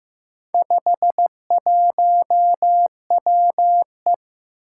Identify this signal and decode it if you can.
Morse code, heard as a keyed tone, about 15 wpm, 700 Hz. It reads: 51WE
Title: Four Challenging Letter-Number Combinations 15wpm